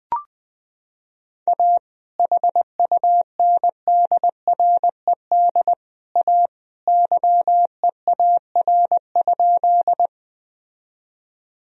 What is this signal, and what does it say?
Morse code, heard as a keyed tone, about 21 wpm, 700 Hz. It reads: A HUNDRED A YEAR?
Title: Sentences from Top 1000 Words - Review Part 2 - 20wpm